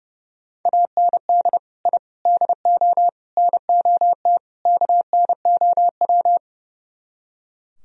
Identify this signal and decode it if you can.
Morse code, heard as a keyed tone, about 34 wpm, 700 Hz. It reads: AND I DO NOT KNOW
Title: Speed-Racing - Sets of 5 Words 30wpm